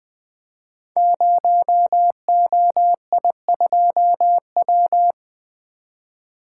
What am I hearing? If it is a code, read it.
Morse code, heard as a keyed tone, about 21 wpm, 700 Hz. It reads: 0OI2W
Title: Code Groups - 5 Letter:Numbers - 20wpm